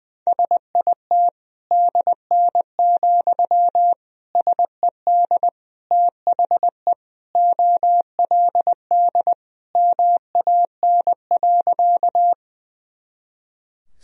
Morse code, heard as a keyed tone, about 20 wpm, 700 Hz. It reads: SIT DN, SED THE OLD MAN.
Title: Sentences from Top 500 Words and Abbreviations 20wpm